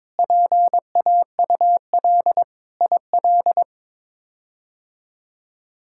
Morse code, heard as a keyed tone, about 22 wpm, 700 Hz. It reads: PAUL IL
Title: K1USN SST QSO Practice - 22wpm